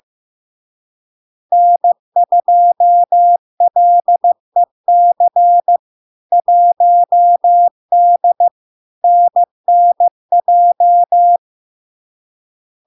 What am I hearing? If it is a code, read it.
Morse code, heard as a keyed tone, about 15 wpm, 700 Hz. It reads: N2LEC 1D NNJ
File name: LARK-FD-2022-Log-in-CW-15wpm-700-Hz